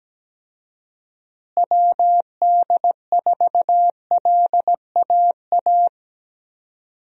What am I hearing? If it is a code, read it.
Morse code, heard as a keyed tone, about 17 wpm, 700 Hz. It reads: WD4LAA
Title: USA Call Signs - 6 Characters - Repeated 3x - 17wpm